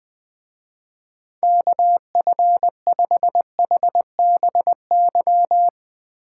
Morse code, heard as a keyed tone, about 20 wpm, 700 Hz. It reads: KF5HBY
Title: Speed Racing - USA Call Signs 17wpm